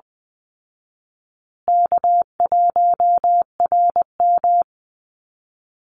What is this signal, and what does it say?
Morse code, heard as a keyed tone, about 20 wpm, 700 Hz. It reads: K1RM
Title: Call Signs Repeated 2x - 20wpm